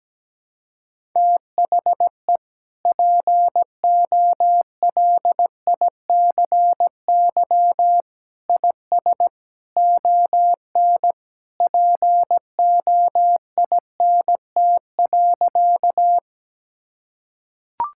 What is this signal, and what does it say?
Morse code, heard as a keyed tone, about 17 wpm, 700 Hz. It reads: THE POLICY IS ON POINT.
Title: Sentences from Top 1000 Words - Review Part 5 - 17wpm